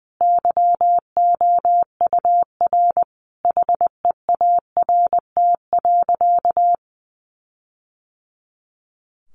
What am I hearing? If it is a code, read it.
Morse code, heard as a keyed tone, about 20 wpm, 700 Hz. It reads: YOUR HEART.
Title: Sentences from Top 1400 Words 20wpm